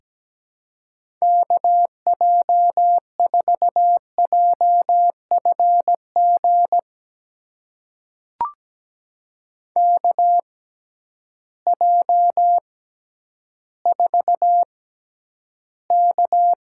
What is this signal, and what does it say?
Morse code, heard as a keyed tone, about 17 wpm, spaced as for 10 wpm, 700 Hz. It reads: KJ4JFG KJ4K
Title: USA Call Signs - 6 Characters - 6x Character Spacing - 17wpm